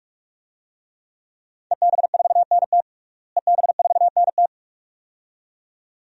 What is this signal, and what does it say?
Morse code, heard as a keyed tone, about 44 wpm, 700 Hz. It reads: EB4NT EB4NT
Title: DXCC-100 Call Signs - 5 Characters - Repeated 2x - 45wpm